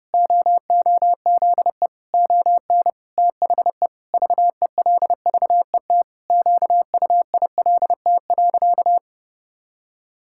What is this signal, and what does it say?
Morse code, heard as a keyed tone, about 32 wpm, 700 Hz. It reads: OOZE ON THE VELVET QUILT.
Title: Pangram Challenge v2 - 30wpm